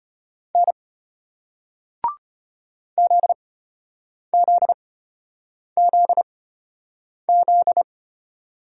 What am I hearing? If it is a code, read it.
Morse code, heard as a keyed tone, about 35 wpm, spaced as for 8 wpm, 700 Hz. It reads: N ZZZZ